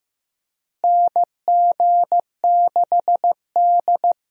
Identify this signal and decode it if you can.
Morse code, heard as a keyed tone, about 15 wpm, 700 Hz. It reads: NG6D
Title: USA Call Signs - 4 Characters - Repeated 3x - 15wpm